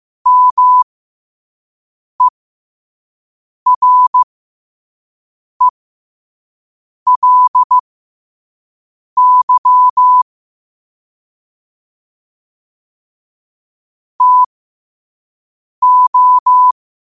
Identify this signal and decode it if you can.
Morse code, heard as a keyed tone, about 15 wpm, spaced as for 5 wpm, 1000 Hz. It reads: MERELY TO
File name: qotdfiles / 5wpm